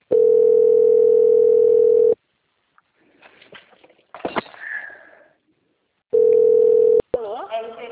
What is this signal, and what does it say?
Ringback tone (North America), 440 + 480 Hz.